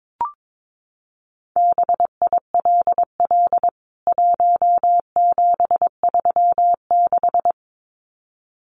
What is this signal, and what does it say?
Morse code, heard as a keyed tone, about 22 wpm, 700 Hz. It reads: BILL 1736
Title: CWT Exchange 22wpm